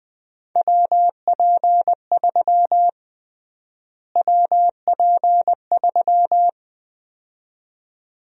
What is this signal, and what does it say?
Morse code, heard as a keyed tone, about 20 wpm, 700 Hz. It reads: WP3 WP3
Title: USA Call Sign Prefixes - Repeated 2x - 20wpm